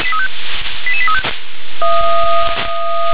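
Five-tone selective call (ZVEI), heard as quick runs of five tones, tones of about 80 ms. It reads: 88835
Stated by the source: ZVEI five-tone call received by SDR.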